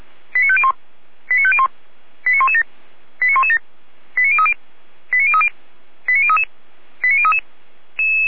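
Five-tone selective call (ZVEI), heard as quick runs of five tones, tones of about 70 ms. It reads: 89501 89501 89117 89117 89030 89030 89033 89033
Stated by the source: ZVEI five-tone selective call received on an SDR.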